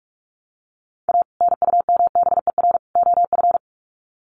Morse code, heard as a keyed tone, about 44 wpm, 700 Hz. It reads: A NUMBER OF
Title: Sets of 3 Words 45wpm